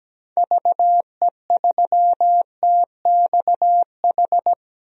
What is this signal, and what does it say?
Morse code, heard as a keyed tone, about 17 wpm, 700 Hz. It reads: VE3TXH
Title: DXCC-100 Call Signs - 6 Characters - Repeated 3x - 17wpm